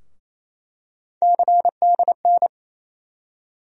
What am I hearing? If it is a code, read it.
Morse code, heard as a keyed tone, about 28 wpm, 700 Hz. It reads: CDN
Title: USA Call Sign Postfixes - 3 Letters 28wpm